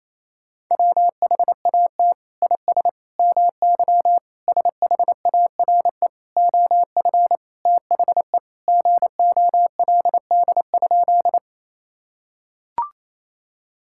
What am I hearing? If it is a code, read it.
Morse code, heard as a keyed tone, about 27 wpm, 700 Hz. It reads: WHAT IS MY SHARE OF THE GOLD?
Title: Sentences from Top 1000 Words - Review Part 6 - 28wpm